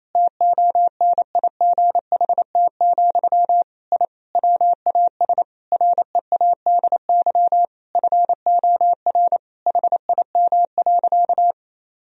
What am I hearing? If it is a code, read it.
Morse code, heard as a keyed tone, about 28 wpm, 700 Hz. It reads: TONIGHT, I WAS READY FOR HIM.
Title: Sentences from Top 1000 Words - Review Part 3 - 28wpm